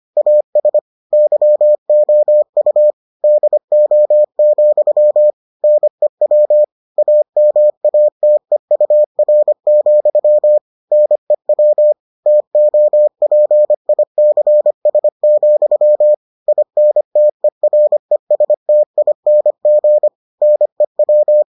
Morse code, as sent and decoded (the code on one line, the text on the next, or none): .- ... / -.-- --- ..- / -.. --- --..-- / -. .-- / .- -- .- - . ..- .-. --..-- / -. .-- / - --- .--. .. -.-. ... --..-- / .. -. - . .-. ... - .. -. --. / -. .--
AS YOU DO, NEW AMATEUR, NEW TOPICS, INTERESTING NEW